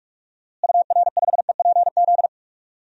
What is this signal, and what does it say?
Morse code, heard as a keyed tone, about 46 wpm, 700 Hz. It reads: UR5EPZ
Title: DXCC-100 Call Signs - 6 Characters - Repeated 2x - 45wpm